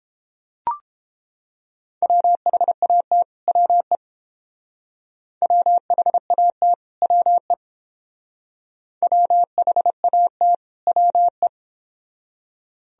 Morse code, heard as a keyed tone, about 28 wpm, 700 Hz. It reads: WHAT WE WHAT WE WHAT WE